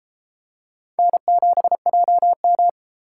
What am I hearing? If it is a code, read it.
Morse code, heard as a keyed tone, about 32 wpm, 700 Hz. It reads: N7JM